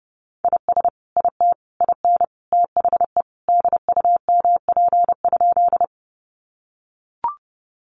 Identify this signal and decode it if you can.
Morse code, heard as a keyed tone, about 30 wpm, 700 Hz. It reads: IS IT IN THE DUMP?
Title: Sentences from Top 1300 Words 30wpm